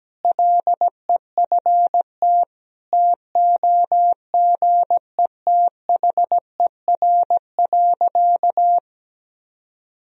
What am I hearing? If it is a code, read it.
Morse code, heard as a keyed tone, about 17 wpm, 700 Hz. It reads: LEFT TOGETHER.
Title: Sentences from Top 400 Words and Abbreviations 17wpm